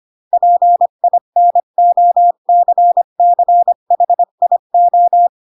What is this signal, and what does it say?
Morse code, heard as a keyed tone, about 25 wpm, 700 Hz. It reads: PINOCCHIO
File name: Pinocchio / Pinocchio ch6 20 25 ik1hge